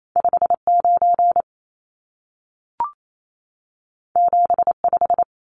Morse code, heard as a keyed tone, about 28 wpm, 700 Hz. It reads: 59 75
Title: Numbers - 2-Digit 28wpm